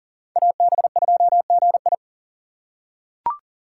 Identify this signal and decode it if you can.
Morse code, heard as a keyed tone, about 39 wpm, 700 Hz. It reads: AB2GI
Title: K1USN SST QSO Practice - 40wpm